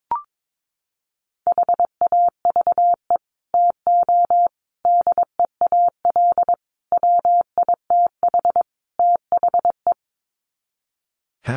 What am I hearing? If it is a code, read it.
Morse code, heard as a keyed tone, about 22 wpm, 700 Hz. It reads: HAVE TO DEAL WITH THE
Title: Sets of 5 Words Encore2 22wpm